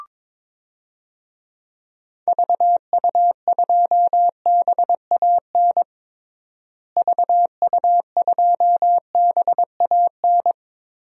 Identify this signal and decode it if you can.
Morse code, heard as a keyed tone, about 22 wpm, 700 Hz. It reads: VU2BAN VU2BAN
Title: DXCC-100 Call Signs - 6 Characters - Repeated 3x - 22wpm